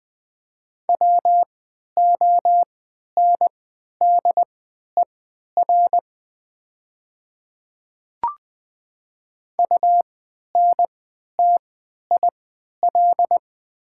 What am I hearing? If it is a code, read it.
Morse code, heard as a keyed tone, about 20 wpm, spaced as for 11 wpm, 700 Hz. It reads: WONDER UNTIL
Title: 3x Character Spacing - Top 500 Words - 20wpm